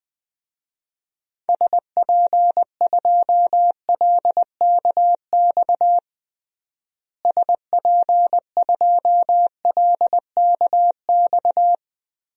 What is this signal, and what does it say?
Morse code, heard as a keyed tone, about 20 wpm, 700 Hz. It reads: SP2LKX SP2LKX
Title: DXCC-100 Call Signs - 6 Characters - Repeated 3x - 20wpm